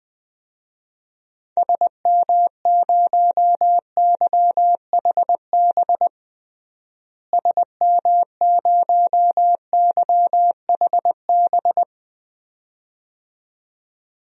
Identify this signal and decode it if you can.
Morse code, heard as a keyed tone, about 19 wpm, 700 Hz. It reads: SM0YHB SM0YHB
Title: DXCC-100 Call Signs - 6 Characters - Repeated 2x - 20wpm